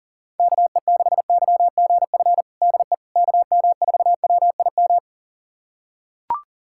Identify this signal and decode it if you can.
Morse code, heard as a keyed tone, about 40 wpm, 700 Hz. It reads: KE6YGF DE KM4WIM
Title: USA Callsign DE Callsign - 6 Characters - 40wpm